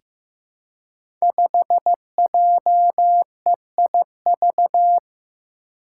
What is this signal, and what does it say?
Morse code, heard as a keyed tone, about 15 wpm, 700 Hz. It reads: 5JEIV
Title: Five Challenging Letter-Number Combinations 15wpm